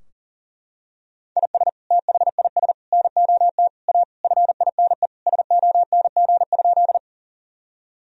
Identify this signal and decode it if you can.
Morse code, heard as a keyed tone, about 40 wpm, 700 Hz. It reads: IS THIS NOT A FINE SONG?
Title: Sentences from Top 1000 Words - Review Part 2 - 40wpm